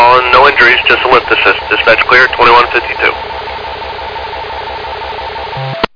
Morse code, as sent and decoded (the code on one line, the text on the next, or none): none